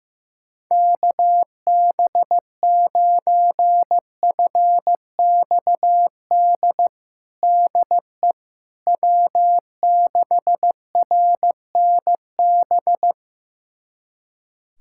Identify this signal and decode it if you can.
Morse code, heard as a keyed tone, about 16 wpm, 700 Hz. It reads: KB9FXD DE W6RNB
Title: USA Callsign DE Callsign 15wpm